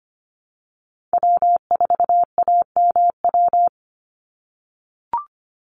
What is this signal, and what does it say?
Morse code, heard as a keyed tone, about 25 wpm, 700 Hz. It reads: W4AMW
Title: NA SOTA Chaser Callsigns 25wpm